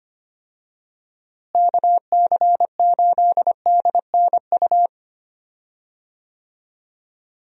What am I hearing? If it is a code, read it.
Morse code, heard as a keyed tone, about 25 wpm, 700 Hz. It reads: KC8DNU
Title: USA Call Signs - 6 Characters - Repeated 3x - 25wpm